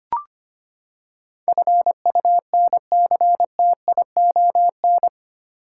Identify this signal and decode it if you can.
Morse code, heard as a keyed tone, about 25 wpm, 700 Hz. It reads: FUNCTION
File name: Eight Letter Words - 25wpm